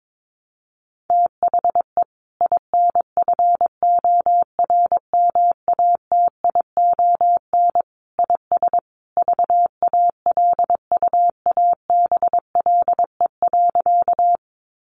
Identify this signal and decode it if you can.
Morse code, heard as a keyed tone, about 22 wpm, 700 Hz. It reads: THE INFORMATION IS VALUABLE.
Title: Sentences from Top 1100 Words 22wpm